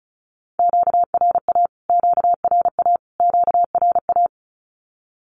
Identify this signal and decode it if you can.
Morse code, heard as a keyed tone, about 34 wpm, 700 Hz. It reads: QRA QRA QRA
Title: Q-Codes Repeated 3x 35wpm